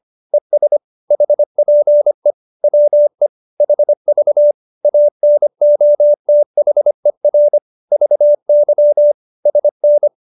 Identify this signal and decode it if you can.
Morse code, heard as a keyed tone, about 25 wpm, 590 Hz. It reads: ES HPE WE HV ANOTHER VY SN